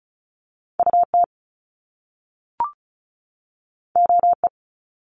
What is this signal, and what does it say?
Morse code, heard as a keyed tone, about 34 wpm, 700 Hz. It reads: UT OE
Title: Top Two Letter Combinations in Words 35wpm